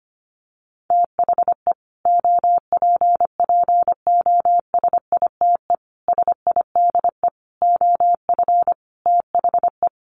Morse code, heard as a keyed tone, about 25 wpm, 700 Hz. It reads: THE OPPOSITE SIDE OF THE
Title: Sets of 5 Words Encore 25wpm